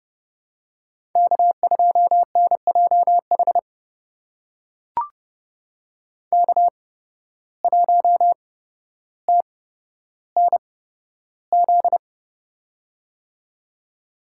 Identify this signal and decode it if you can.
Morse code, heard as a keyed tone, about 31 wpm, spaced as for 13 wpm, 700 Hz. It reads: K2NJH K1TNZ
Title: USA Call Signs - 5 Characters - 8x Character Spacing - 30wpm